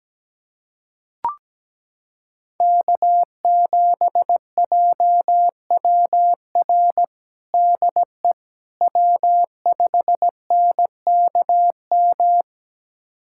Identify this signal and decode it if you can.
Morse code, heard as a keyed tone, about 17 wpm, 700 Hz. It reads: K7JWR DE W5NKM
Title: USA Callsign DE Callsign 17wpm